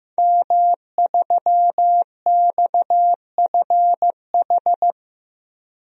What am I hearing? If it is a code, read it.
Morse code, heard as a keyed tone, about 15 wpm, 700 Hz. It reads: M3XFH
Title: DXCC-100 Call Signs - 5 Characters - Repeated 2x - 15wpm